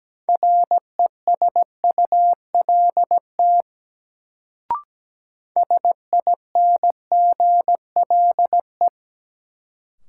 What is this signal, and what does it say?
Morse code, heard as a keyed tone, about 17 wpm, 700 Hz. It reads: RESULT SINGLE
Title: Sentences from Top 700 Words - New Word Drill - 17wpm